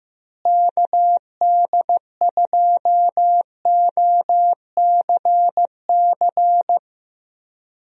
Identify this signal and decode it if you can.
Morse code, heard as a keyed tone, about 15 wpm, 700 Hz. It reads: KD2OCC
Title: USA Call Signs - 6 Characters - Repeated 2x - 15wpm